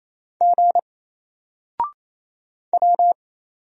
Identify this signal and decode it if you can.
Morse code, heard as a keyed tone, about 27 wpm, 700 Hz. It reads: G W
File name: Single-Letter-Number-28wpm